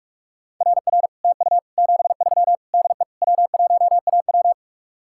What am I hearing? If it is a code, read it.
Morse code, heard as a keyed tone, about 55 wpm, 700 Hz. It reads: RR TU 73 DE W1AW
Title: Speed-Racing Common QSO Phrases - 45wpm